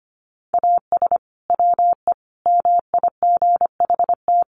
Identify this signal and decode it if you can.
Morse code, heard as a keyed tone, about 25 wpm, 700 Hz. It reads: AS WE MIGHT
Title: Sentences from Top 1100 Words 25wpm